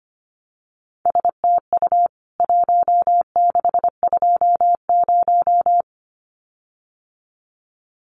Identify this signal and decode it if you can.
Morse code, heard as a keyed tone, about 25 wpm, 700 Hz. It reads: STU 1620
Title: CWT Exchange 25wpm